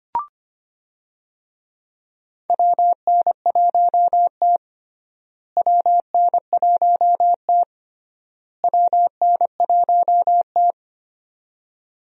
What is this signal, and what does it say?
Morse code, heard as a keyed tone, about 25 wpm, 700 Hz. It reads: WN1T WN1T WN1T
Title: USA Call Signs - 4 Characters - Repeated 3x - 25wpm